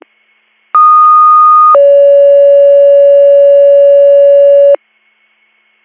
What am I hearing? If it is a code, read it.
Two-tone page, heard as a one-second tone, then a three-second tone, about 1230 then 560 Hz. It is heard from 0.7 to 4.8 s.